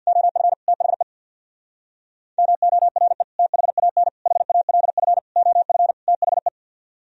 Morse code, heard as a keyed tone, about 63 wpm, 700 Hz. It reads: OF THE MORE THAN HALF OF THE